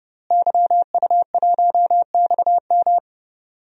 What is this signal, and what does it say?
Morse code, heard as a keyed tone, about 31 wpm, 700 Hz. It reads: YU1XM
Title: DXCC-100 Call Signs - 5 Characters - Repeated 3x - 30wpm